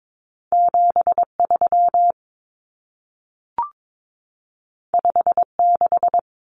Morse code, heard as a keyed tone, about 22 wpm, 700 Hz. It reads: 73 56
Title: Numbers - 2-Digit 22wpm